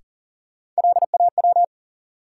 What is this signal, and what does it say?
Morse code, heard as a keyed tone, about 40 wpm, 700 Hz. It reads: LAW
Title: Sentences from Top 600 Words - New Word Drill - 40wpm